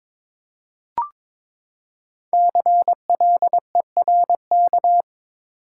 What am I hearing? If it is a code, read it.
Morse code, heard as a keyed tone, about 22 wpm, 700 Hz. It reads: CLERK
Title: Five Letter Words - 22wpm